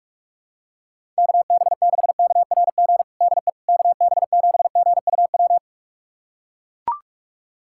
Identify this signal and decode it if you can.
Morse code, heard as a keyed tone, about 42 wpm, 700 Hz. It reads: KB6KRG DE KB7GUW
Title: USA Callsign DE Callsign - 6 Characters - 45wpm